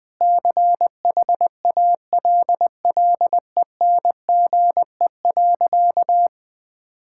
Morse code, heard as a keyed tone, about 20 wpm, 700 Hz. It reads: CHALLENGE.
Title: Sentences with 5-Word Phrases - 20wpm